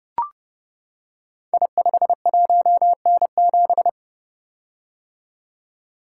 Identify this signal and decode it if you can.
Morse code, heard as a keyed tone, about 29 wpm, 700 Hz. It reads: I51N7
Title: Code Groups - 5 Letter:Numbers - 30wpm